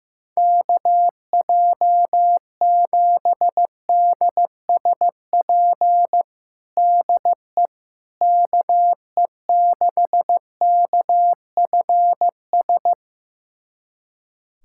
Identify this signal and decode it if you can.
Morse code, heard as a keyed tone, about 15 wpm, 700 Hz. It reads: KJ7DSP DE KE6KFS
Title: USA Callsign DE Callsign - 6 Characters - 15wpm